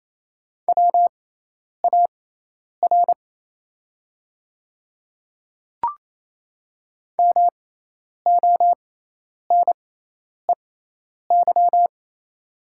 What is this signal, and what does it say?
Morse code, heard as a keyed tone, about 28 wpm, spaced as for 10 wpm, 700 Hz. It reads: WAR MONEY
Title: 6x Character Spacing - Top 500 Words - 28wpm